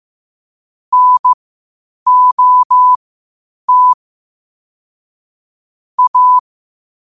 Morse code, heard as a keyed tone, about 15 wpm, spaced as for 8 wpm, 1000 Hz. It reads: NOT A
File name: qotdfiles / 8wpm